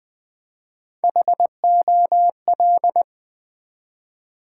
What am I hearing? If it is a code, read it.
Morse code, heard as a keyed tone, about 20 wpm, 700 Hz. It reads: HOL
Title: Build A Word - 1 Letter At A Time - 3 to 5-Letter Words - 20wpm